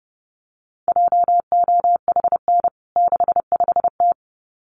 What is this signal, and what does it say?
Morse code, heard as a keyed tone, about 30 wpm, 700 Hz. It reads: JOHN 65T
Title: CWT Exchange 30wpm